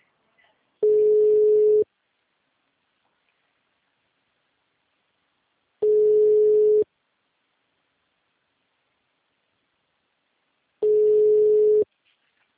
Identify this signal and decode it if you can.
Ringback tone (Europe, 425 Hz).
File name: out- -2012- - - .35637